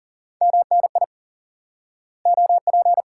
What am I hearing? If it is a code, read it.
Morse code, heard as a keyed tone, about 38 wpm, 700 Hz. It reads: MNI OP
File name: IQE Course 3 - Lesson 89 - YAGI - Two Elements - 40wpm